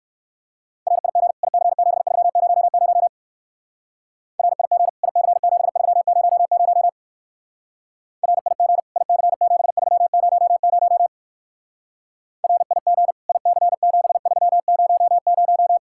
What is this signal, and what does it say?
Morse code, heard as a keyed tone, about 54 wpm, 700 Hz. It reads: RIG IC7300 RIG IC7300 RIG IC7300 RIG IC7300
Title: Speed-Racing Common QSO Phrases - 45wpm